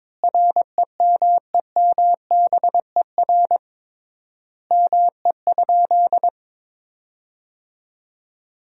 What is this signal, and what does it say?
Morse code, heard as a keyed tone, about 22 wpm, 700 Hz. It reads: REMEMBER ME?
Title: 3x Word-Spacing - Sentences from Top 400 Words - 22wpm